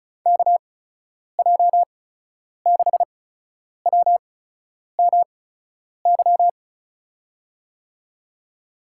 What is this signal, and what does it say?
Morse code, heard as a keyed tone, about 34 wpm, spaced as for 10 wpm, 700 Hz. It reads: KJ6WMY
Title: USA Call Signs - 6 Characters - 8x Character Spacing - 35wpm